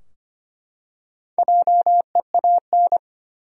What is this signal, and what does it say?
Morse code, heard as a keyed tone, about 25 wpm, 700 Hz. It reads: JEAN
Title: USA Common Names - Max 4 Letters - 25wpm